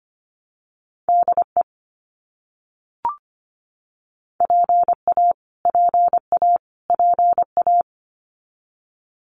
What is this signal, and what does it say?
Morse code, heard as a keyed tone, about 25 wpm, 700 Hz. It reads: DE PA PA PA
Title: US States Repeated 3x 25wpm